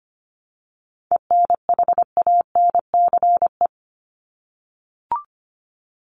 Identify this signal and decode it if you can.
Morse code, heard as a keyed tone, about 25 wpm, 700 Hz. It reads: ENHANCE
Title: Top 1251-1500 Words 25wpm